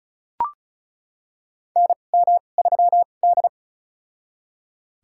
Morse code, heard as a keyed tone, about 34 wpm, spaced as for 25 wpm, 700 Hz. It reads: NM3D
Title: USA Call Signs - 4 Characters - 2x Character Spacing - 35wpm